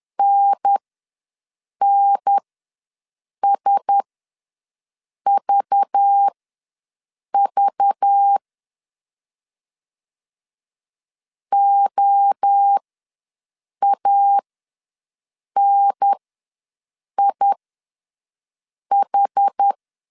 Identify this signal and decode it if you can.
Morse code, heard as a keyed tone, about 10 wpm, spaced as for 6 wpm, 800 Hz. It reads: NNSVV OANIH
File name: LEZIONI / 5 LBVC / 5 LBV ws 10wpm 3645